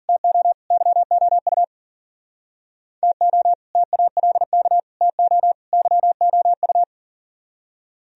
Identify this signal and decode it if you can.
Morse code, heard as a keyed tone, about 51 wpm, 700 Hz. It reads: TO YOU TO TALK TO YOU